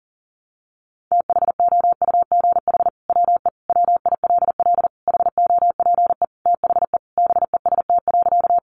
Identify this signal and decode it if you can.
Morse code, heard as a keyed tone, about 40 wpm, 700 Hz. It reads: THOUGH WE WILL HOPE THE BEST.
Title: Sentences from Top 1000 Words 40wpm